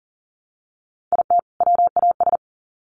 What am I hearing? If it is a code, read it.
Morse code, heard as a keyed tone, about 40 wpm, 700 Hz. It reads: IT WAS
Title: Sets of 2 Words 40wpm